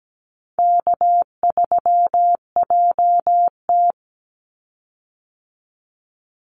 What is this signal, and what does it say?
Morse code, heard as a keyed tone, about 17 wpm, 700 Hz. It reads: K3JT
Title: Easy Call Signs - 17wpm